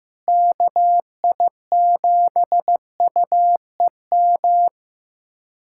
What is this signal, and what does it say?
Morse code, heard as a keyed tone, about 15 wpm, 700 Hz. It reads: KI7UEM
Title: USA Callsign DE Callsign - 6 Characters - 15wpm